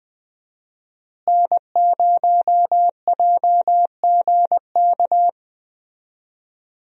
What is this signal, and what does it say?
Morse code, heard as a keyed tone, about 20 wpm, 700 Hz. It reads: N0JGK
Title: USA Call Signs - 5 Characters - Repeated 2x - 20wpm